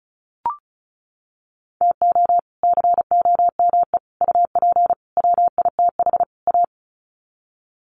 Morse code, heard as a keyed tone, about 34 wpm, 700 Hz. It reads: TO COME UP WITH A
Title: Sets of 5 Words 35wpm